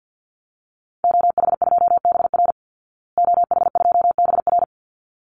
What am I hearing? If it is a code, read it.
Morse code, heard as a keyed tone, about 50 wpm, 700 Hz. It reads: OH2BR OH2BR
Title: Call Signs Repeated 3x - 50wpm